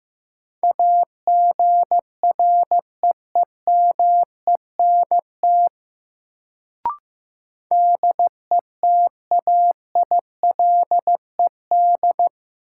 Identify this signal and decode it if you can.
Morse code, heard as a keyed tone, about 15 wpm, 700 Hz. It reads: AGREEMENT DETAILED
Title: Sentences from Top 1500 Words - New Word Drill - 15wpm